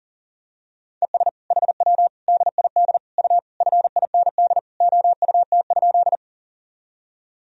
Morse code, heard as a keyed tone, about 40 wpm, 700 Hz. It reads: ES HW DID U FIND OUT?
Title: Sentences from Top 100 Words and Abbreviations 40wpm